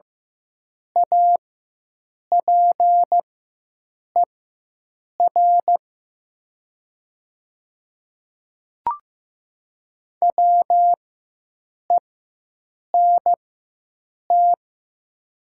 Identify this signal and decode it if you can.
Morse code, heard as a keyed tone, about 15 wpm, spaced as for 7 wpm, 700 Hz. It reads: APER WENT
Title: 4x Character Spacing - Top 300 Words - 15wpm